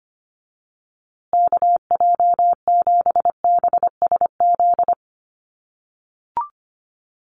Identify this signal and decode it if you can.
Morse code, heard as a keyed tone, about 25 wpm, 700 Hz. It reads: KJ7BSZ
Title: USA Call Signs Repeated 2x 25wpm